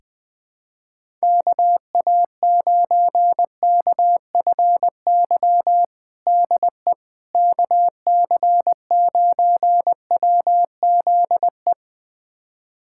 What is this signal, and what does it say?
Morse code, heard as a keyed tone, about 21 wpm, 700 Hz. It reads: KA9KFY DE KC9WZE
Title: USA Callsign DE Callsign - 6 Characters - 20wpm